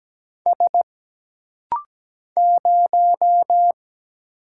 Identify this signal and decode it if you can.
Morse code, heard as a keyed tone, about 17 wpm, 700 Hz. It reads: S 0
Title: Single Letter-Number Alt - Rapid-Fire 17wpm